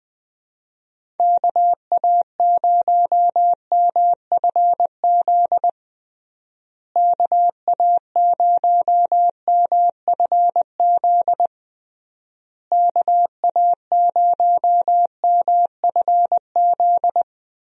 Morse code, heard as a keyed tone, about 20 wpm, 700 Hz. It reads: KA0MFZ KA0MFZ KA0MFZ
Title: USA Call Signs - 6 Characters - Repeated 3x - 20wpm